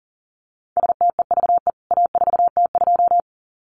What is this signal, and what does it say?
Morse code, heard as a keyed tone, about 40 wpm, 700 Hz. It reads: STEVE A4T2
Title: CWT Exchange 40wpm